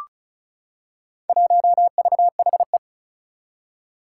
Morse code, heard as a keyed tone, about 36 wpm, 700 Hz. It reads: 1VHE
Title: Four Challenging Letter-Number Combinations 35wpm